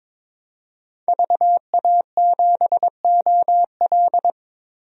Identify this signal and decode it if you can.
Morse code, heard as a keyed tone, about 22 wpm, 700 Hz. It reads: VA7OL
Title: DXCC-100 Call Signs - 5 Characters - 22wpm